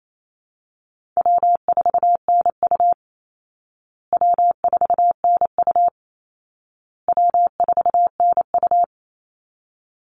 Morse code, heard as a keyed tone, about 28 wpm, 700 Hz. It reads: W4NU W4NU W4NU
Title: Easy Call Signs Repeated 3x - 28wpm